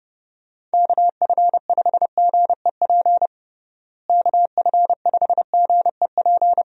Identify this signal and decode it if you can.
Morse code, heard as a keyed tone, about 29 wpm, 700 Hz. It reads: KF5GEP KF5GEP
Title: USA Call Signs - 6 Characters - Repeated 2x - 30wpm